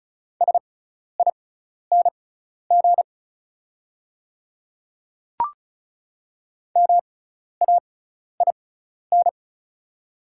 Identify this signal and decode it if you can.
Morse code, heard as a keyed tone, about 36 wpm, spaced as for 12 wpm, 690 Hz. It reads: SING MAIN
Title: 6x Character Spacing - Top 500 Words - 35wpm